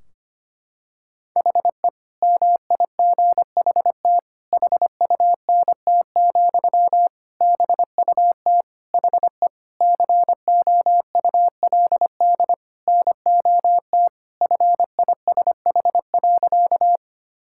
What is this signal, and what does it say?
Morse code, heard as a keyed tone, about 25 wpm, 700 Hz. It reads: HE MIGHT HUNT, BUT HE COULD NOT FISH.
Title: Sentences from Top 1600 Words 25wpm